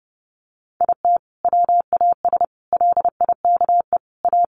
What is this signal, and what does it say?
Morse code, heard as a keyed tone, about 30 wpm, 700 Hz. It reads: IT WAS LIKE A
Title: Sentences from Top 1400 Words 30wpm